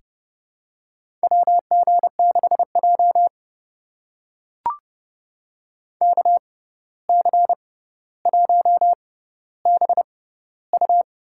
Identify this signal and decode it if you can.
Morse code, heard as a keyed tone, about 31 wpm, spaced as for 15 wpm, 700 Hz. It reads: WG6J KC1BU
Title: USA Call Signs - 6x Character Spacing - 30wpm